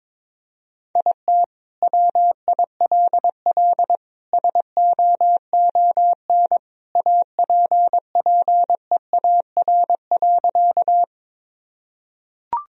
Morse code, heard as a keyed tone, about 22 wpm, 700 Hz. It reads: IT WILL SOON APPEAR.
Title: Sentences from Top 1000 Words - Review Part 2 - 22wpm